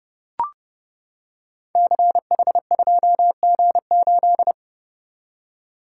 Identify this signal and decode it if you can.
Morse code, heard as a keyed tone, about 30 wpm, 700 Hz. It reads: CH2G8
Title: Code Groups - 5 Letter:Numbers - 30wpm